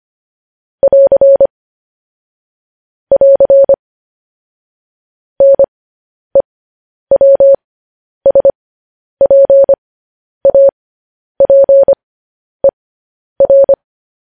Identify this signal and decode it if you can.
Morse code, heard as a keyed tone, about 25 wpm, spaced as for 10 wpm, 550 Hz. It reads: <AR> <AR> NEWSPAPER
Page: news-headlines-in-morse-code-at-10-wpm